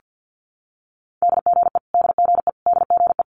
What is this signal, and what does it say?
Morse code, heard as a keyed tone, about 50 wpm, 690 Hz. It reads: DGE DGE DGE
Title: Multi-letter Phonemes Repeated 3x 50wpm